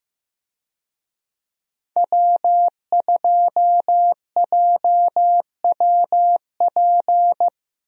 Morse code, heard as a keyed tone, about 15 wpm, 700 Hz. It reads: W2JWP
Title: USA Call Signs - 5 Characters - 15wpm